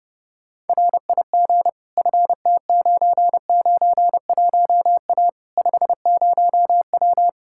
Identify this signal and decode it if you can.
Morse code, heard as a keyed tone, about 32 wpm, 700 Hz. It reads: RIG FT991A 50W
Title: Practice QSO - Exchange 2 - Info - 30wpm